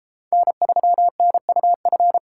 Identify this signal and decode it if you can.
Morse code, heard as a keyed tone, about 33 wpm, 700 Hz. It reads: N3NUF